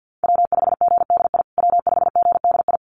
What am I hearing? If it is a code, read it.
Morse code, heard as a keyed tone, about 50 wpm, 700 Hz. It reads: W5GDI W5GDI
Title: USA Call Signs Repeated 2x 50wpm